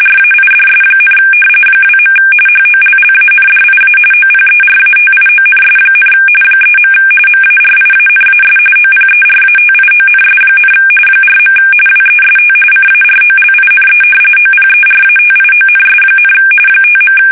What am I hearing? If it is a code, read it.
RTTY, about 45.45 baud. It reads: RAIDS, 2. HOLLANDIA OPERATIONS, 3. MARIANAS OPERATIONS, 4. WEST CAROLINE OPERATIONS, 5.
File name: fsk850 / battleshipnj